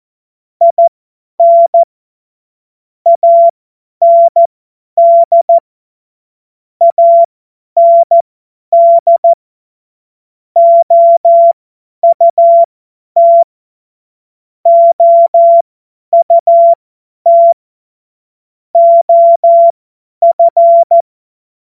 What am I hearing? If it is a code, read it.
Morse code, heard as a keyed tone, about 14 wpm, spaced as for 10 wpm, 680 Hz. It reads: IN AND AND OUT OUT OF
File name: Morsekurs / Witze / Witz 14. Dezember 2021, 50 F70